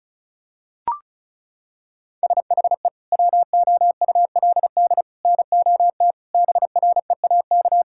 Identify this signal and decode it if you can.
Morse code, heard as a keyed tone, about 34 wpm, 700 Hz. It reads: SHE WOULD NOT BREAK
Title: Sentences from Top 1000 Words - Review Part 6 - 35wpm